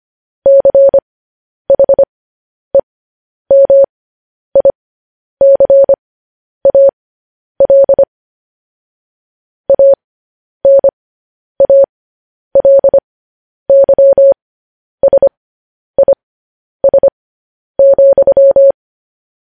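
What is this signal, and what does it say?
Morse code, heard as a keyed tone, about 25 wpm, spaced as for 10 wpm, 550 Hz. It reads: CHEMICAL ANALYSIS,
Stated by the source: Morse code news headlines broadcast.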